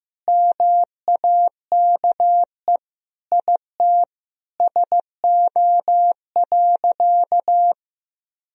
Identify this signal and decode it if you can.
Morse code, heard as a keyed tone, about 16 wpm, 700 Hz. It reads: MAKE IT SO.
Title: Sentences from Top 1000 Words - Review Part 3 - 15wpm